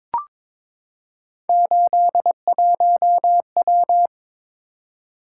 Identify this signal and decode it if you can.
Morse code, heard as a keyed tone, about 22 wpm, 700 Hz. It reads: 81W
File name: practice / 22wpm / Three Letter-Number Combinations - 22wpm